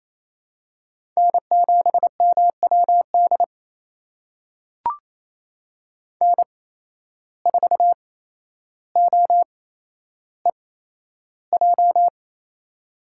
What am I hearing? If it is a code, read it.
Morse code, heard as a keyed tone, about 28 wpm, spaced as for 12 wpm, 700 Hz. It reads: N7MWD N4OEJ
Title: USA Call Signs - 8x Character Spacing - 28wpm